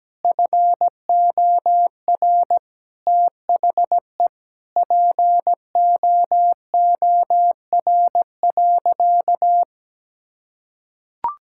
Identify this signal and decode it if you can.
Morse code, heard as a keyed tone, about 17 wpm, 700 Hz. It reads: FOR THE POOR.
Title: Sentences from Top 1600 Words 17wpm